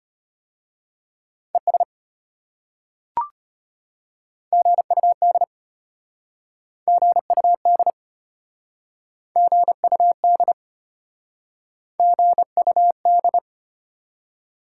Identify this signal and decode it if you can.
Morse code, heard as a keyed tone, about 32 wpm, 700 Hz. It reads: ES GUD GUD GUD GUD